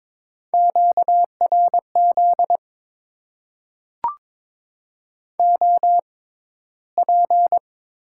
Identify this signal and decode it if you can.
Morse code, heard as a keyed tone, about 22 wpm, spaced as for 13 wpm, 700 Hz. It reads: QRZ OP
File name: Top 90 QSO Elements - 6x Character Spacing - 22wpm